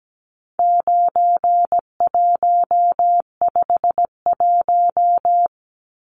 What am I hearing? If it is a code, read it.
Morse code, heard as a keyed tone, about 17 wpm, 700 Hz. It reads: 9151
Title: Numbers - 4-Digit 17wpm